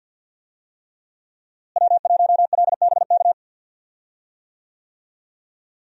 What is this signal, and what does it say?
Morse code, heard as a keyed tone, about 52 wpm, 700 Hz. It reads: W1LBK
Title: USA Call Signs - 5 Characters - Repeated 3x - 50wpm